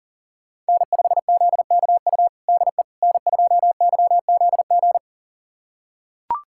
Morse code, heard as a keyed tone, about 40 wpm, 700 Hz. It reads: N5ZKU DE N2YZG
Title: USA Callsign DE Callsign - 5 Characters - 40wpm